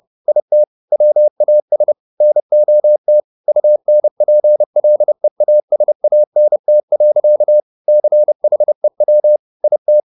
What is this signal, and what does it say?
Morse code, heard as a keyed tone, about 30 wpm, 600 Hz. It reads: IT WAS NOT UNPLEASANT. CHEW IT
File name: the old man and the sea / the old man and the sea 1 30wpm